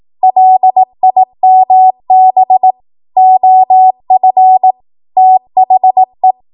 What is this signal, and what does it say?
Morse code, heard as a keyed tone, about 18 wpm, 750 Hz. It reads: LIMB OF THE
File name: Morse / 18WPM